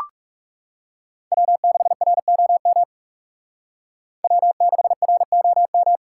Morse code, heard as a keyed tone, about 38 wpm, 700 Hz. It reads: W6ROM W6ROM